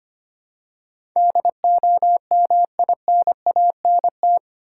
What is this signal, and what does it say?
Morse code, heard as a keyed tone, about 25 wpm, 700 Hz. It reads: DOMINANT
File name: 8 Letter Words - First 3 Letters Given - 25wpm